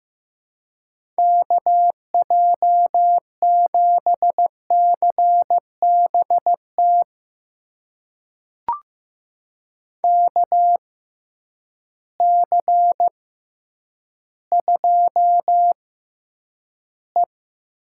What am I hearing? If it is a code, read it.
Morse code, heard as a keyed tone, about 15 wpm, spaced as for 9 wpm, 700 Hz. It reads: KJ7CBT KC2E
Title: USA Call Signs - 6 Characters - 6x Character Spacing - 15wpm